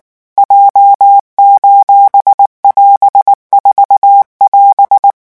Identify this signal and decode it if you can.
Morse code, heard as a keyed tone, about 19 wpm, 800 Hz. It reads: J:<AS>4<AS>
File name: morse mp3 / revision15-vitesse 18 mots